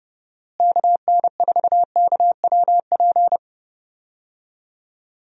Morse code, heard as a keyed tone, about 32 wpm, 700 Hz. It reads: KN4KWP
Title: USA Call Signs - 6 Characters - Repeated 2x - 30wpm